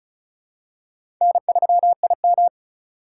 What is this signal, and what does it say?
Morse code, heard as a keyed tone, about 36 wpm, 700 Hz. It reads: N3IM
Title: USA Call Signs - 4 Characters - 2x Character Spacing - 35wpm